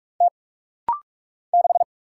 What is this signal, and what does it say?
Morse code, heard as a keyed tone, about 44 wpm, 700 Hz. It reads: T 6
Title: Single Letter-Number Alt - Rapid-Fire 45wpm